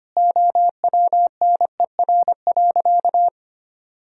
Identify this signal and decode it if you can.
Morse code, heard as a keyed tone, about 25 wpm, 700 Hz. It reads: OWNER.
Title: Sentences from Top 1600 Words 25wpm